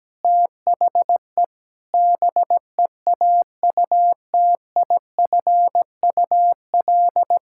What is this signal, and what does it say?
Morse code, heard as a keyed tone, about 17 wpm, 700 Hz. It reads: THE BEAUTIFUL
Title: Pangram Challenge v2 - 17wpm